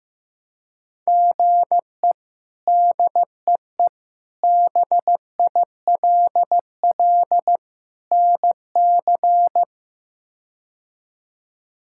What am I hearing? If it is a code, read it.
Morse code, heard as a keyed tone, about 15 wpm, 700 Hz. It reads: GE DEE BILL NC
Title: K1USN SST QSO Practice - 15wpm